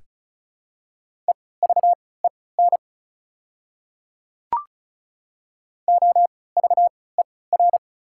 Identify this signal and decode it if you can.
Morse code, heard as a keyed tone, about 34 wpm, spaced as for 20 wpm, 700 Hz. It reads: EVEN OVER
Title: 3x Character Spacing - Top 500 Words - 35wpm